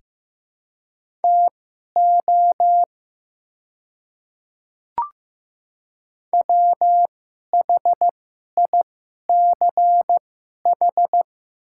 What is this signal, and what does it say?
Morse code, heard as a keyed tone, about 15 wpm, spaced as for 11 wpm, 700 Hz. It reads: TO WHICH
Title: 2x Character Spacing - Top 100 Words - 15wpm